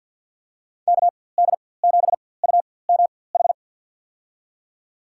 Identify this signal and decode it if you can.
Morse code, heard as a keyed tone, about 46 wpm, spaced as for 23 wpm, 700 Hz. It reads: KD7UMH
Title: USA Call Signs - 6 Characters - 4x Character Spacing - 50wpm